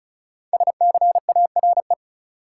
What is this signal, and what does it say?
Morse code, heard as a keyed tone, about 36 wpm, 700 Hz. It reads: SCARE
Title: Sentences from Top 1000 Words - Review Part 5 - New Word Drill - 35wpm